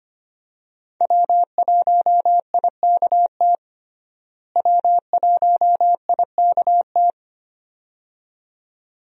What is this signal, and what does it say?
Morse code, heard as a keyed tone, about 25 wpm, 700 Hz. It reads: W1IKT W1IKT
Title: USA Call Signs - 5 Characters - Repeated 2x - 25wpm